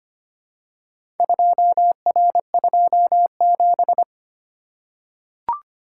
Morse code, heard as a keyed tone, about 25 wpm, 700 Hz. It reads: 2R27
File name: practice / 25wpm / Four Letter-Number Combinations - 25wpm